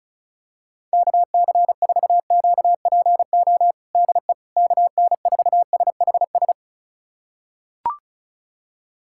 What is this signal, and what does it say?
Morse code, heard as a keyed tone, about 34 wpm, 700 Hz. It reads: KC4QPO DE KN4SHS
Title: USA Callsign DE Callsign 35wpm